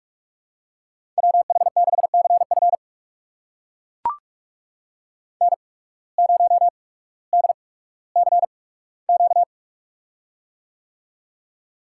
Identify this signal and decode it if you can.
Morse code, heard as a keyed tone, about 43 wpm, spaced as for 19 wpm, 700 Hz. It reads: WH6CF N0DCQ
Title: USA Call Signs - 5 Characters - 8x Character Spacing - 45wpm